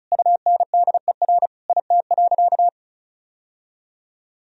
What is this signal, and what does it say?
Morse code, heard as a keyed tone, about 34 wpm, 700 Hz. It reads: UNDER IT.
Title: Sentences from Top 1000 Words - Review Part 1 - 35wpm